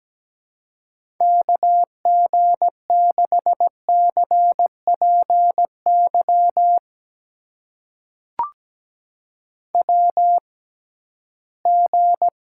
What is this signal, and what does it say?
Morse code, heard as a keyed tone, about 17 wpm, spaced as for 13 wpm, 700 Hz. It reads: KG6CPY WG
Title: USA Call Signs - 6x Character Spacing - 17wpm